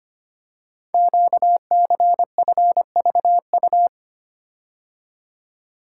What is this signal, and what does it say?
Morse code, heard as a keyed tone, about 25 wpm, 700 Hz. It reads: QCFVU
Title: Code Groups - 5 Letters - 25wpm